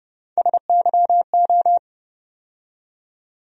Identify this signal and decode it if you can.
Morse code, heard as a keyed tone, about 31 wpm, 700 Hz. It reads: SYO
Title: USA Call Sign Postfixes - 3 Letters 30wpm